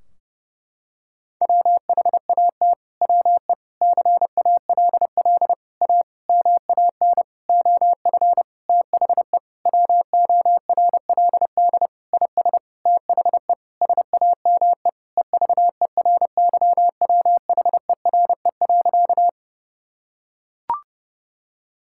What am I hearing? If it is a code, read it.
Morse code, heard as a keyed tone, about 30 wpm, 700 Hz. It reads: WHAT WE CALL A MAN OF THE WORLD IS THE SAME EVERYWHERE.
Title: Sentences from Top 1600 Words 30wpm